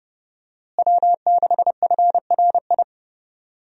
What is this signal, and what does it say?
Morse code, heard as a keyed tone, about 30 wpm, 700 Hz. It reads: W6FRI
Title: USA Call Signs - 5 Characters - Repeated 3x - 30wpm